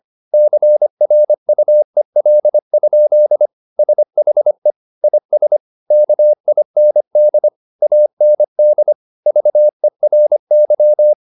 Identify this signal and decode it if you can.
Morse code, heard as a keyed tone, about 25 wpm, 600 Hz. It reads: CRUEL? SHE IS KIND AND VERY
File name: the old man and the sea part 1 25wpm